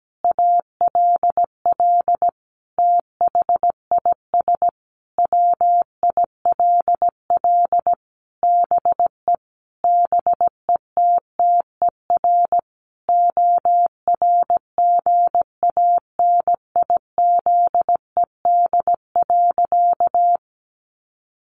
Morse code, heard as a keyed tone, about 17 wpm, 700 Hz. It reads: ALL THIS WILL BE BETTER ORGANIZED.
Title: Sentences from Top 1300 Words 17wpm